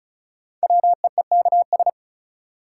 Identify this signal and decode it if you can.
Morse code, heard as a keyed tone, about 36 wpm, 700 Hz. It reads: WEEKS
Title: Sets of 5 Words - New Word Drill - 35wpm